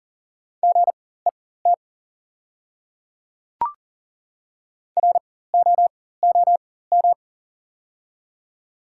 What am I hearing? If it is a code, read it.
Morse code, heard as a keyed tone, about 40 wpm, spaced as for 19 wpm, 700 Hz. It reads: GET ROOM
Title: 4x Character Spacing - Top 300 Words - 40wpm